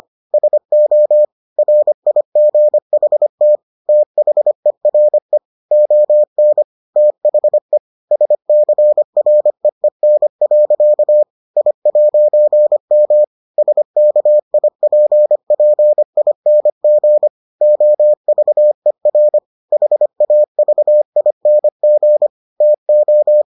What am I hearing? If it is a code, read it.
Morse code, heard as a keyed tone, about 25 wpm, 600 Hz. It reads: SO RIGHT THERE ON THE SCREEN. I'M SKIPPING OVER HAVING TO